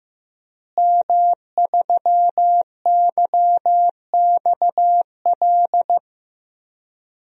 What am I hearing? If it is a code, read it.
Morse code, heard as a keyed tone, about 15 wpm, 700 Hz. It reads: M3YXL
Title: DXCC-100 Call Signs - 5 Characters - Repeated 2x - 15wpm